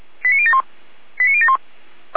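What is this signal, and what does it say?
Five-tone selective call (ZVEI), heard as quick runs of five tones, tones of about 70 ms. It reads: 89071 89071
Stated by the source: ZVEI five-tone call 89140 received by SDR, from the feuerwehr-orlamuende.org fire brigade server.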